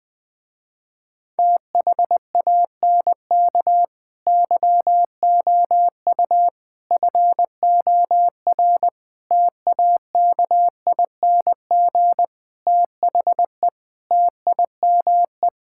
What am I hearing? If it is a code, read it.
Morse code, heard as a keyed tone, about 20 wpm, 700 Hz. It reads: THANK YOU FOR TAKING THE TIME
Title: Sentences with 9-Word Phrases - Spoken Beginning - 20wpm